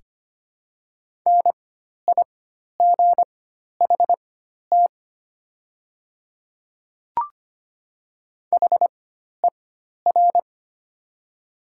Morse code, heard as a keyed tone, about 25 wpm, spaced as for 12 wpm, 700 Hz. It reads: NIGHT HER
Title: 4x Character Spacing - Top 300 Words - 25wpm